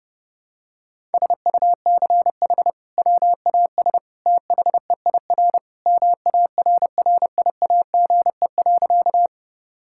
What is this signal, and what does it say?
Morse code, heard as a keyed tone, about 30 wpm, 700 Hz. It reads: SUCH WAS THEIR MARRIAGE.
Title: Sentences from Top 1000 Words - Review Part 5 - 30wpm